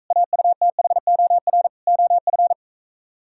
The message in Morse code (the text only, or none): AUTHOR OF